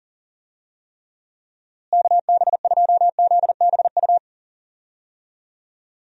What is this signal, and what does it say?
Morse code, heard as a keyed tone, about 40 wpm, 700 Hz. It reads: KB2ZBU
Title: USA Call Signs - 6 Characters - Repeated 3x - 40wpm